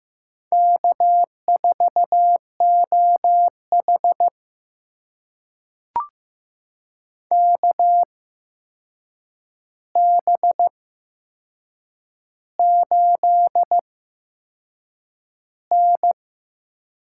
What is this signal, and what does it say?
Morse code, heard as a keyed tone, about 15 wpm, spaced as for 6 wpm, 700 Hz. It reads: K4OH KB8N
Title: USA Call Signs - 8x Character Spacing - 15wpm